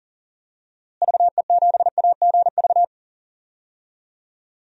Morse code, heard as a keyed tone, about 37 wpm, 700 Hz. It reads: VE7AGV
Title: DXCC-100 Call Signs - 6 Characters - Repeated 3x - 40wpm